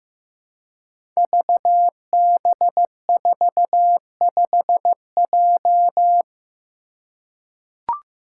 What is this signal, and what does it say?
Morse code, heard as a keyed tone, about 15 wpm, 700 Hz. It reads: VB45J
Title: Code Groups - 5 Letter:Numbers - 15wpm